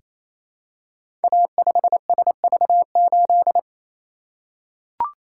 Morse code, heard as a keyed tone, about 27 wpm, 700 Hz. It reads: A5SV8
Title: Code Groups - 5 Letter:Numbers - 28wpm